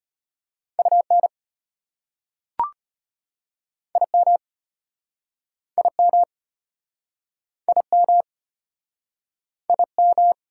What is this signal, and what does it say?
Morse code, heard as a keyed tone, about 36 wpm, 700 Hz. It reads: UN IM IM IM IM